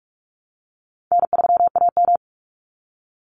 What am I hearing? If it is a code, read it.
Morse code, heard as a keyed tone, about 46 wpm, 700 Hz. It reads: N3AM
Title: Call Signs - 45wpm